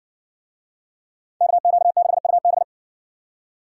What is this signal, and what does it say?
Morse code, heard as a keyed tone, about 60 wpm, 700 Hz. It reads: KQ6UB